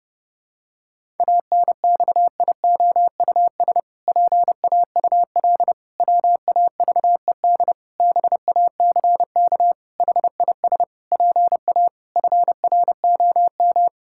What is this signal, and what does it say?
Morse code, heard as a keyed tone, about 32 wpm, 700 Hz. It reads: ANXIOUS PAUL WAVED BACK HIS PA FROM
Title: Pangram Challenge v2 - 30wpm